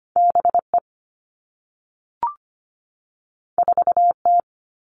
Morse code, heard as a keyed tone, about 25 wpm, 700 Hz. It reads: B E 4 T
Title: Two Challenging Letter-Number Combinations 25wpm